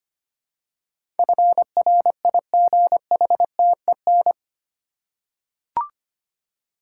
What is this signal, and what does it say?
Morse code, heard as a keyed tone, about 25 wpm, 700 Hz. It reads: FRIGHTEN
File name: 8 Letter Words - First 3 Letters Given - 25wpm